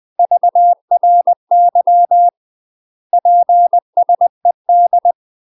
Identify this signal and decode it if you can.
Morse code, heard as a keyed tone, about 20 wpm, 700 Hz. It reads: VRY PSED
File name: UK MASTER QSOs TRAINING / 20WPM / UK MASTER QSOs 0055